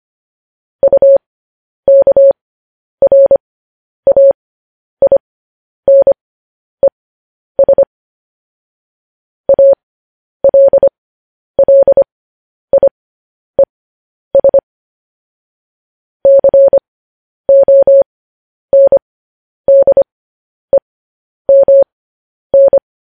Morse code, heard as a keyed tone, about 25 wpm, spaced as for 10 wpm, 550 Hz. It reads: UKRAINES ALLIES CONDEMN
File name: podcast / News headlines-1x-25x10